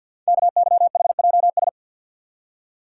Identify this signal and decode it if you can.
Morse code, heard as a keyed tone, about 53 wpm, 700 Hz. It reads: KYHJS
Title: Code Groups - 5 Letters - 50wpm